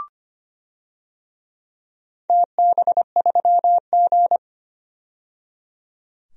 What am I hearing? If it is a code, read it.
Morse code, heard as a keyed tone, about 25 wpm, 700 Hz. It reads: TB3G
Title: DXCC-100 Call Signs - 4 Characters - 25wpm